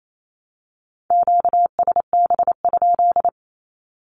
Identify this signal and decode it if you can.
Morse code, heard as a keyed tone, about 28 wpm, 700 Hz. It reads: QSB?
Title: Q-Codes Repeated 3x 28wpm